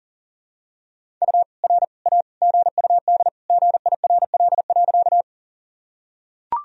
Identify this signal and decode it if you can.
Morse code, heard as a keyed tone, about 40 wpm, 700 Hz. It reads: U R A GUD GIRL.
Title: Sentences from Top 400 Words and Abbreviations 40wpm